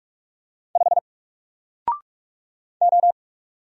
Morse code, heard as a keyed tone, about 46 wpm, 700 Hz. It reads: 5 O
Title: Single Letter-Number Alt 45wpm